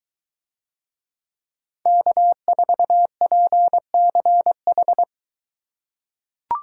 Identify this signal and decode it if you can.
Morse code, heard as a keyed tone, about 23 wpm, 700 Hz. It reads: K4PCH